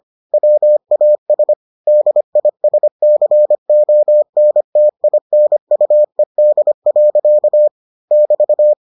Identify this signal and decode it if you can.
Morse code, heard as a keyed tone, about 25 wpm, 600 Hz. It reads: WAS DISCONTINUED. <BT>